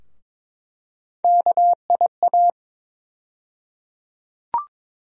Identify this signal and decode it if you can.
Morse code, heard as a keyed tone, about 22 wpm, 700 Hz. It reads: KIA
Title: USA Call Sign Postfixes - 3 Letters 22wpm